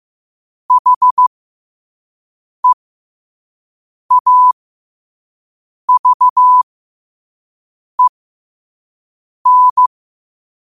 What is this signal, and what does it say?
Morse code, heard as a keyed tone, about 15 wpm, spaced as for 5 wpm, 1000 Hz. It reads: HEAVEN
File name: qotdfiles / 5wpm